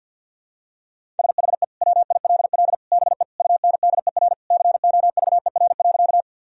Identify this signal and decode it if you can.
Morse code, heard as a keyed tone, about 53 wpm, 700 Hz. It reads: SHE WILL BE UNDER COVER.
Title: Sentences from Top 1000 Words - Review Part 1 - 50wpm